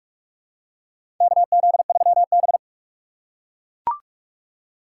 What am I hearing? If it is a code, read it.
Morse code, heard as a keyed tone, about 43 wpm, 700 Hz. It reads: KZ3B
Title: USA Call Signs - 4 Characters - 2x Character Spacing - 45wpm